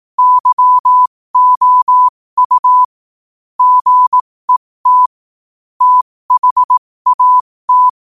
Morse code, heard as a keyed tone, about 18 wpm, spaced as for 15 wpm, 1000 Hz. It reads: YOU GET THAT
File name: qotdfiles / 15wpm